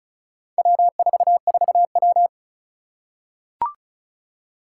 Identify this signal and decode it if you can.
Morse code, heard as a keyed tone, about 35 wpm, 700 Hz. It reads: W44W
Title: Four Challenging Letter-Number Combinations 35wpm